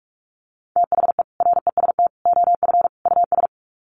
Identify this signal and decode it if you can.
Morse code, heard as a keyed tone, about 44 wpm, 700 Hz. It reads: THE REST OF US
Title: Sets of 4 Words 45wpm